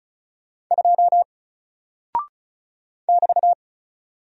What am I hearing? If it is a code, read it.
Morse code, heard as a keyed tone, about 35 wpm, 700 Hz. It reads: 2 <BT>
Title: Single Character (Letter:Number:Punctuation:Prosign) 35wpm